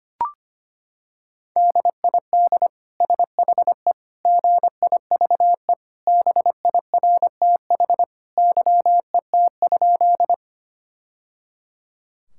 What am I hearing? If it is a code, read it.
Morse code, heard as a keyed tone, about 25 wpm, 700 Hz. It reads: DID SHE GIVE BIRTH YET?
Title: Sentences from Top 1000 Words - Review Part 5 - 25wpm